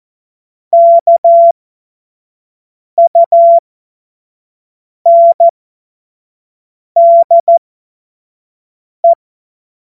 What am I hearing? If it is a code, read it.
Morse code, heard as a keyed tone, about 14 wpm, spaced as for 5 wpm, 680 Hz. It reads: KUNDE
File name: Morsekurs / Witze / Witz 30 Januar 2024 25 F70